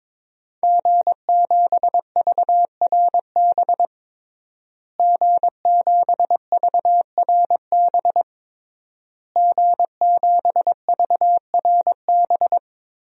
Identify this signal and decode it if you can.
Morse code, heard as a keyed tone, about 21 wpm, 700 Hz. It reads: G7VRB G7VRB G7VRB
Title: DXCC-100 Call Signs - 5 Characters - Repeated 3x - 22wpm